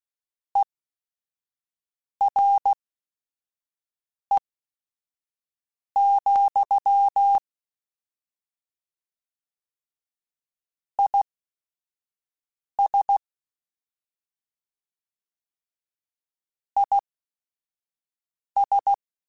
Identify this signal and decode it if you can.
Morse code, heard as a keyed tone, about 16 wpm, spaced as for 5 wpm, 800 Hz. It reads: ERE, IS IS